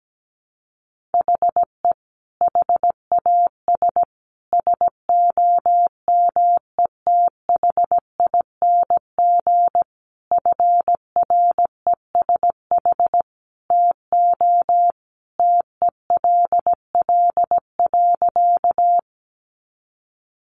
Morse code, heard as a keyed tone, about 17 wpm, 700 Hz. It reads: HE HAS SOMETHING FRESH TO TELL.
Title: Sentences from Top 1200 Words 17wpm